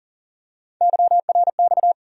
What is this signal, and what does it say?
Morse code, heard as a keyed tone, about 40 wpm, 700 Hz. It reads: YRX
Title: USA Call Sign Postfixes - 3 Letters 40wpm